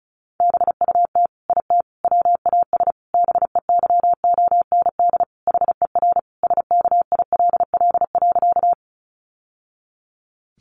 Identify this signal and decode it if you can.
Morse code, heard as a keyed tone, about 34 wpm, 700 Hz. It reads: BUT IT WAS BEYOND HER SKILL.
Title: Sentences from Top 1200 Words 35wpm